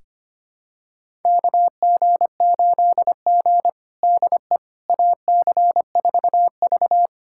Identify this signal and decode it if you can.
Morse code, heard as a keyed tone, about 25 wpm, 700 Hz. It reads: KG8G DE AC4V
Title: USA Callsign DE Callsign - 4 Characters - 25wpm